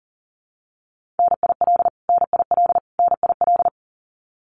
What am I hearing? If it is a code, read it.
Morse code, heard as a keyed tone, about 40 wpm, 700 Hz. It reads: NIL NIL NIL
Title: Abbreviations Repeated 3x - 40wpm